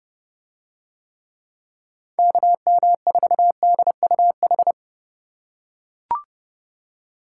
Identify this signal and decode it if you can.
Morse code, heard as a keyed tone, about 27 wpm, 700 Hz. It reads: KM4DUH